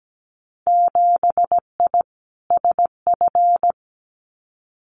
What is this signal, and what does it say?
Morse code, heard as a keyed tone, about 17 wpm, 700 Hz. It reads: 7I SF
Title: Winter Field Day Exchanges Encore 17wpm